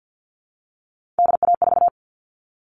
Morse code, heard as a keyed tone, about 50 wpm, 700 Hz. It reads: DA4
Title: All Cut Numbers - 3-Digit 50wpm